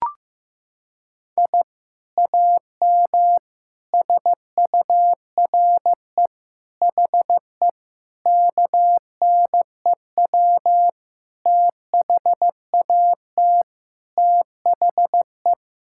Morse code, heard as a keyed tone, about 16 wpm, 700 Hz. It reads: I AM SURE HE KNEW THAT THE
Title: Sentences from Top 1000 Words - Review Part 6 - 15wpm